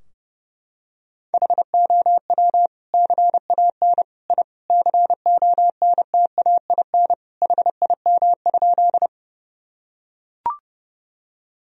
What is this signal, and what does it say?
Morse code, heard as a keyed tone, about 30 wpm, 700 Hz. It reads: HOW CAN I CONTAIN HIM?
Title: Sentences from Top 1000 Words - Review Part 2 - 30wpm